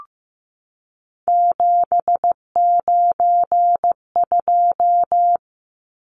Morse code, heard as a keyed tone, about 15 wpm, 700 Hz. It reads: 792
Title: Numbers - 3-Digit 15wpm